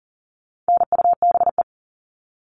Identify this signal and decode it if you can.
Morse code, heard as a keyed tone, about 40 wpm, 700 Hz. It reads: NUBE
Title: All Cut Numbers - 4-Digit 40wpm